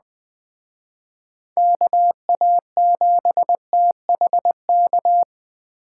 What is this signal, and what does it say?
Morse code, heard as a keyed tone, about 20 wpm, 700 Hz. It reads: KA7THK
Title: Speed Racing - USA 2x3 Call Signs 17wpm